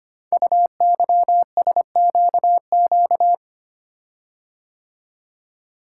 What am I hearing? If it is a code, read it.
Morse code, heard as a keyed tone, about 25 wpm, 700 Hz. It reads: UYSQQ
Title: Code Groups - 5 Letters - 25wpm